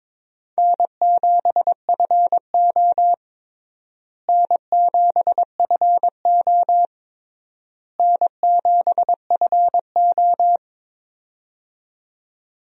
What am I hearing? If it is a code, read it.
Morse code, heard as a keyed tone, about 22 wpm, 700 Hz. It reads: N7FO N7FO N7FO
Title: USA Call Signs - 4 Characters - Repeated 3x - 22wpm